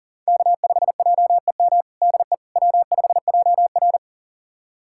Morse code, heard as a keyed tone, about 40 wpm, 700 Hz. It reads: K5JEM DE W5JR
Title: USA Callsign DE Callsign 40wpm